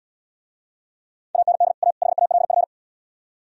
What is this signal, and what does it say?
Morse code, heard as a keyed tone, about 77 wpm, 700 Hz. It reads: AND A HALF